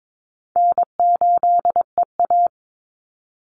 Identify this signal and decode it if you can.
Morse code, heard as a keyed tone, about 22 wpm, 700 Hz. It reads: N8EA
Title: Easy Call Signs Repeated 2x - 22wpm